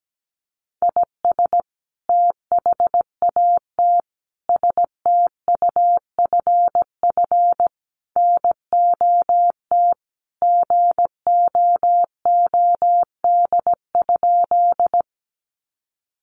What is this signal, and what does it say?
Morse code, heard as a keyed tone, about 17 wpm, 700 Hz. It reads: IS THAT STUFF NOT GOOD?
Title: Sentences from Top 1400 Words 17wpm